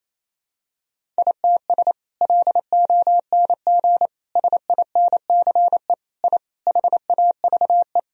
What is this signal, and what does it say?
Morse code, heard as a keyed tone, about 27 wpm, 700 Hz. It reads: ITS LONG SINCE I HAVE
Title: Sentences from Top 1000 Words - Review Part 1 - 28wpm